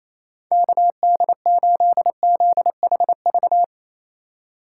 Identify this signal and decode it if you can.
Morse code, heard as a keyed tone, about 34 wpm, 700 Hz. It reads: KD8ZHV
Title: Speed Racing - USA 2x3 Call Signs 28wpm